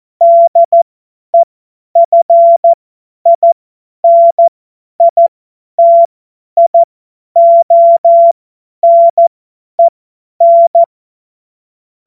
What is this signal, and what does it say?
Morse code, heard as a keyed tone, about 14 wpm, spaced as for 10 wpm, 680 Hz. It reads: DEFINITIONEN
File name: Morsekurs / Witze / Witz 14. Juni 2022 50 F70